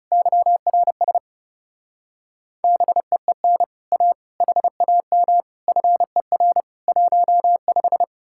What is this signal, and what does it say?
Morse code, heard as a keyed tone, about 35 wpm, 700 Hz. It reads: YRS BEEN A HAM FER 15
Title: Speed-Racing Common QSO Phrases - 30wpm